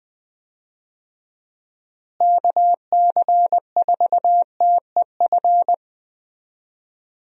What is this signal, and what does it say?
Morse code, heard as a keyed tone, about 20 wpm, 700 Hz. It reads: KC4TEF
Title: USA Call Signs - 6 Characters - Repeated 2x - 20wpm